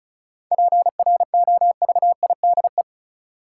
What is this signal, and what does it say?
Morse code, heard as a keyed tone, about 36 wpm, 700 Hz. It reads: PROVIDE
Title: Sentences from Top 1000 Words - Review Part 2 - New Word Drill - 35wpm